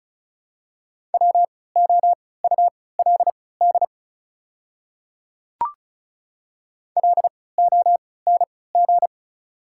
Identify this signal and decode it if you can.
Morse code, heard as a keyed tone, about 36 wpm, spaced as for 20 wpm, 700 Hz. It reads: WOULD LONG
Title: 3x Character Spacing - Top 200 Words - 35wpm